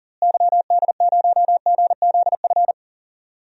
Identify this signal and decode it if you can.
Morse code, heard as a keyed tone, about 40 wpm, 700 Hz. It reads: YD0GZF
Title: DXCC-100 Call Signs - 6 Characters - Repeated 3x - 40wpm